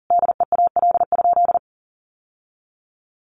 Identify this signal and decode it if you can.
Morse code, heard as a keyed tone, about 40 wpm, 700 Hz. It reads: DEAL?
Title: Sentences from Top 1400 Words 40wpm